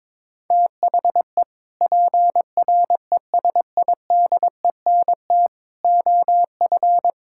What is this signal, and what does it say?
Morse code, heard as a keyed tone, about 22 wpm, 700 Hz. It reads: THE PRESIDENT OF
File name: practice / 22wpm / Sentences with 3-Word Phrases - 22wpm